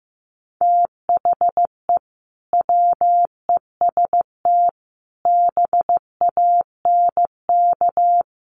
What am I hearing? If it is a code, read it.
Morse code, heard as a keyed tone, about 15 wpm, 700 Hz. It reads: THE WEST BANK
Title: Sets of 5 Words 15wpm